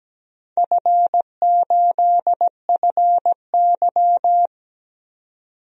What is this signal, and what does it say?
Morse code, heard as a keyed tone, about 17 wpm, 700 Hz. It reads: F8FY
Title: DXCC-100 Call Signs - 4 Characters - 17wpm